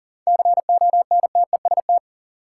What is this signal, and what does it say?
Morse code, heard as a keyed tone, about 40 wpm, 700 Hz. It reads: CONTEST
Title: Sentences from Top 1600 Words - New Word Drill - 40wpm